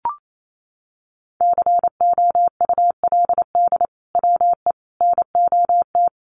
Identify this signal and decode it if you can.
Morse code, heard as a keyed tone, about 28 wpm, 700 Hz. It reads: COULD WE NOT
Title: Sentences from Top 1400 Words 28wpm